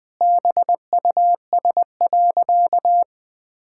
Morse code, heard as a keyed tone, about 21 wpm, 700 Hz. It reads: BUS.
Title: Sentences from Top 1000 Words - Review Part 6 - 20wpm